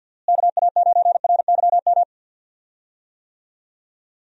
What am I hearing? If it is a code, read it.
Morse code, heard as a keyed tone, about 52 wpm, 700 Hz. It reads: KA9RYM
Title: USA Call Signs - 6 Characters - Repeated 3x - 50wpm